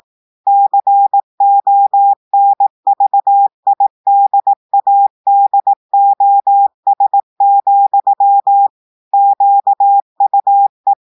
Morse code, heard as a keyed tone, about 18 wpm, 800 Hz. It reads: CONVIDADOS, QUE
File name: audios / portugais / portugais texte en clair 1 vitesse 18 mots minutes